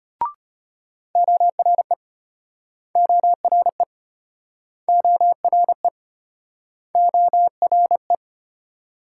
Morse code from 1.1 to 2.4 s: ORE

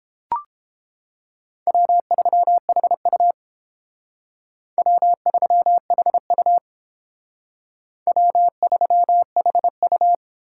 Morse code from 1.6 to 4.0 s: W3HU

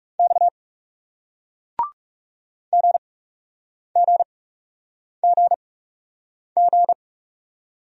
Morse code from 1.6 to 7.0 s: GGGG

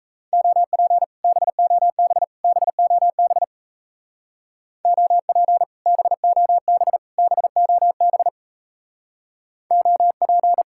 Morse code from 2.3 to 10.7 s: BOB OP BOB BOB OP

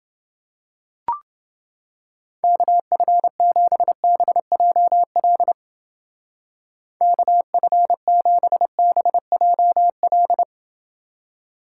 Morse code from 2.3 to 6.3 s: KF7BJL

